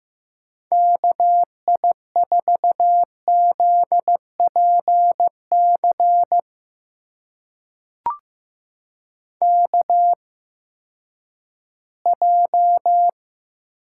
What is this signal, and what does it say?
Morse code, heard as a keyed tone, about 15 wpm, spaced as for 10 wpm, 700 Hz. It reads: KI4ZPC KJ
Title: USA Call Signs - 6 Characters - 8x Character Spacing - 15wpm